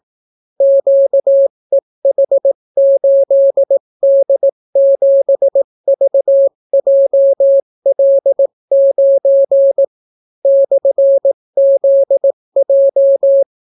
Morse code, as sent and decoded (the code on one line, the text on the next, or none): --.- . .... ---.. -.. --... ...- .--- .-.. ----. / -..-. --.. .---
QEH8D7VJL9 /ZJ